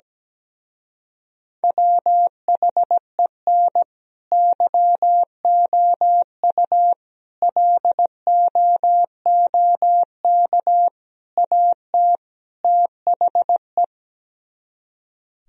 Morse code, heard as a keyed tone, about 17 wpm, 700 Hz. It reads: WHEN YOU LOOK AT THE
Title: Speed-Racing - Sets of 5 Words 17wpm